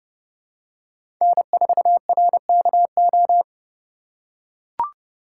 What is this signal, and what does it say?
Morse code, heard as a keyed tone, about 31 wpm, 700 Hz. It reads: N4RKO
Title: USA Call Signs - 5 Characters - 6x Character Spacing - 30wpm